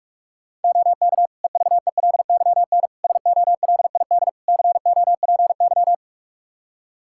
Morse code, heard as a keyed tone, about 46 wpm, 700 Hz. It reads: OK EVELYN SOLID COPY
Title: Practice QSO - Exchange 3 - More Info - 45wpm